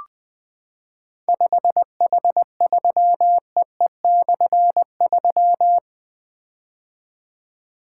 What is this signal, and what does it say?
Morse code, heard as a keyed tone, about 20 wpm, 700 Hz. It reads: 5H3EE/3
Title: Difficult Call Signs 20wpm